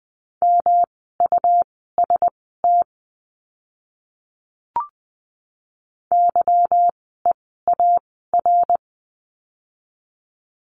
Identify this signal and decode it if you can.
Morse code, heard as a keyed tone, about 20 wpm, spaced as for 14 wpm, 700 Hz. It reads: MUST YEAR
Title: 237 Lesson 38f - Letter q - Words 20wpm